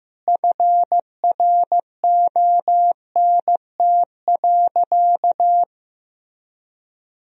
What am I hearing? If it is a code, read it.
Morse code, heard as a keyed tone, about 15 wpm, 700 Hz. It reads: FRONT.
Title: Sentences from Top 1000 Words - Review Part 2 - 15wpm